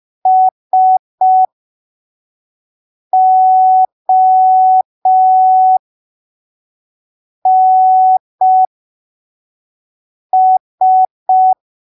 Morse code, heard as a keyed tone, about 5 wpm, spaced as for 3 wpm, 750 Hz. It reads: SONS
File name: arquivos / texto / texto 18 05ppm